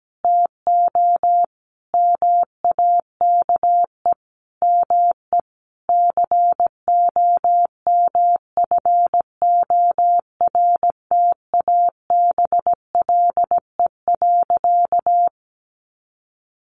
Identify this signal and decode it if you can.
Morse code, heard as a keyed tone, about 17 wpm, 700 Hz. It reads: TO MAKE ME COMFORTABLE.
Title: Sentences from Top 1500 Words 17wpm